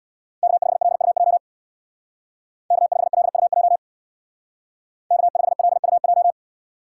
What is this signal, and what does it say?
Morse code, heard as a keyed tone, about 63 wpm, 700 Hz. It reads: K5LUJ K5LUJ K5LUJ